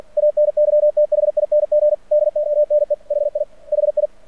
Morse code, heard as a keyed tone, about 49 wpm, 600 Hz. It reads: ANYTHING DTANE HI HI
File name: archived / CRSnet